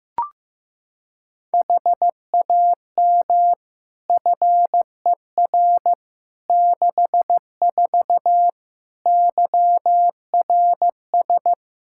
Morse code, heard as a keyed tone, about 15 wpm, 700 Hz. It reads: HAM FER 64 YRS
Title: Practice QSO - Exchange 3 - More Info - 15wpm